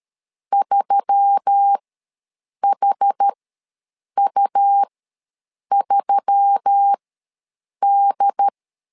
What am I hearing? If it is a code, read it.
Morse code, heard as a keyed tone, about 13 wpm, spaced as for 7 wpm, 800 Hz. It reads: 3HU3D
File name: LEZIONI / 4 Q1S3 ws 12wpm 2038